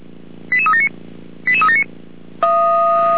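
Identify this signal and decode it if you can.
Five-tone selective call (ZVEI), heard as quick runs of five tones, tones of about 70 ms. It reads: 88379 88379
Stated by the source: ZVEI five-tone selective call received with an SDR.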